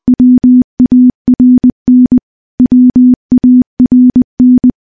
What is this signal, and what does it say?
Morse code, heard as a keyed tone, about 21 wpm, 250 Hz. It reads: WARN WARN
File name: morse / morse-training-20WPM